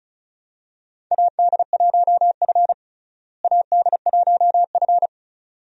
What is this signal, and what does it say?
Morse code, heard as a keyed tone, about 36 wpm, 700 Hz. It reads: AD1F AD1F
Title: USA Call Signs - 4 Characters - Repeated 2x - 35wpm